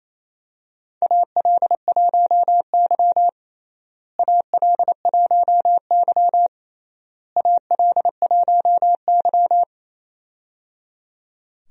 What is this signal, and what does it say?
Morse code, heard as a keyed tone, about 28 wpm, 700 Hz. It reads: AL1Y AL1Y AL1Y
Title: USA Call Signs - 4 Characters - Repeated 3x - 28wpm